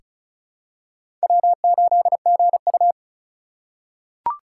Morse code, heard as a keyed tone, about 34 wpm, 700 Hz. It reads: W8GU
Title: K1USN SST QSO Practice - 35wpm